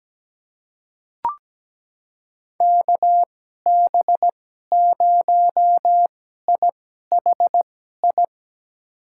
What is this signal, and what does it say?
Morse code, heard as a keyed tone, about 17 wpm, spaced as for 12 wpm, 700 Hz. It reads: KB0IHI
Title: USA Call Signs - 2x Character Spacing - 17wpm